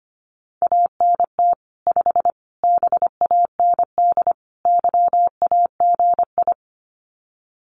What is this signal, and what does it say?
Morse code, heard as a keyed tone, about 25 wpm, 700 Hz. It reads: ANT 5 BAND YAGI
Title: Common QSO Phrases 25wpm